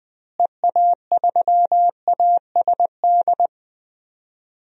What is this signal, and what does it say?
Morse code, heard as a keyed tone, about 21 wpm, 700 Hz. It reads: EA3ASD
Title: DXCC-100 Call Signs - 6 Characters - Repeated 3x - 20wpm